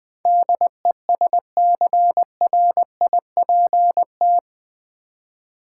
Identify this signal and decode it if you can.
Morse code, heard as a keyed tone, about 20 wpm, 700 Hz. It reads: DESCRIPT
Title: Build A Word - 4 Letters At A Time - 11-Letter Words - 20wpm